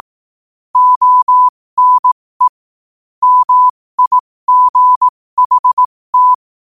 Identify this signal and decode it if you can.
Morse code, heard as a keyed tone, about 18 wpm, spaced as for 15 wpm, 1000 Hz. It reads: ONE MIGHT
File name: qotdfiles / 15wpm